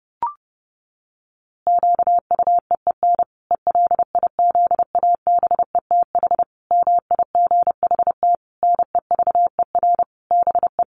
Morse code, heard as a keyed tone, about 30 wpm, 700 Hz. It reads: QUEEN ELIZABETH MIGHT NEVER BE
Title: A Sound Of Thunder 30wpm